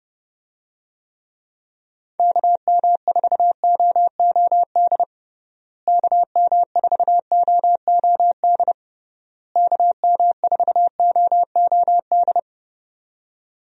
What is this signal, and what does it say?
Morse code, heard as a keyed tone, about 32 wpm, 700 Hz. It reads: KM4OOD KM4OOD KM4OOD
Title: USA Call Signs - 6 Characters - Repeated 3x - 30wpm